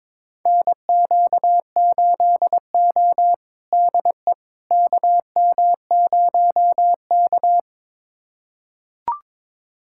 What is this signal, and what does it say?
Morse code, heard as a keyed tone, about 21 wpm, 700 Hz. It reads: NQ8O DE KM0K
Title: USA Callsign DE Callsign - 4 Characters - 22wpm